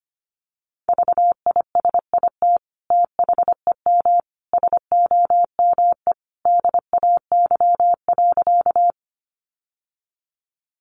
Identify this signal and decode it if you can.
Morse code, heard as a keyed tone, about 25 wpm, 700 Hz. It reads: VISIT THEM SOME DAY.
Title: Sentences from Top 900 Words 25wpm